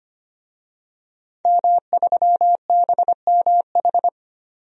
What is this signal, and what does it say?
Morse code, heard as a keyed tone, about 25 wpm, 700 Hz. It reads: M3BMH
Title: DXCC-100 Call Signs - 5 Characters - 25wpm